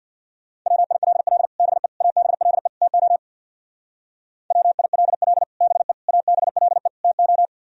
Morse code, heard as a keyed tone, about 55 wpm, 700 Hz. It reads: WILL BE ABLE TO WILL BE ABLE TO
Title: Speed-Racing - Sets of 4 Words 50wpm